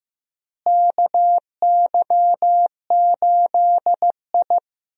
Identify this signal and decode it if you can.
Morse code, heard as a keyed tone, about 15 wpm, 700 Hz. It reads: KY8I
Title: USA Call Signs - 4 Characters - Repeated 3x - 15wpm